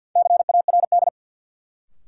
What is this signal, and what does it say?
Morse code, heard as a keyed tone, about 53 wpm, 700 Hz. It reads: CARD
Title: Sentences from Top 600 Words - New Word Drill - 50wpm